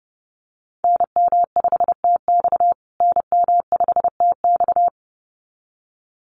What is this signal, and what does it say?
Morse code, heard as a keyed tone, about 30 wpm, 700 Hz. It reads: NM5TX NM5TX
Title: USA Call Signs Repeated 2x 30wpm